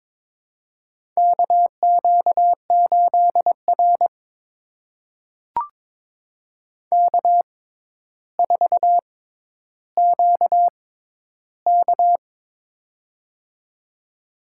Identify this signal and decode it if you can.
Morse code, heard as a keyed tone, about 22 wpm, spaced as for 11 wpm, 700 Hz. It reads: KQ8R K4QK
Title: USA Call Signs - 6x Character Spacing - 22wpm